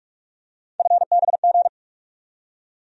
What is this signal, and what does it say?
Morse code, heard as a keyed tone, about 45 wpm, 700 Hz. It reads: FBG
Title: USA Call Sign Postfixes - 3 Letters 45wpm